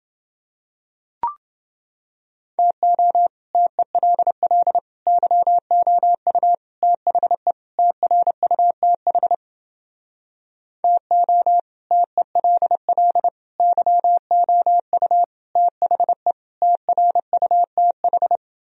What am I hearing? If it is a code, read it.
Morse code, heard as a keyed tone, about 32 wpm, 700 Hz. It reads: TO TELL YOU THE TRUTH TO TELL YOU THE TRUTH